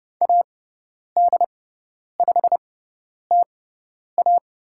Morse code, heard as a keyed tone, about 30 wpm, spaced as for 10 wpm, 700 Hz. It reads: AD5TA
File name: Lesson 17g - Letter u - Call Signs 30wpm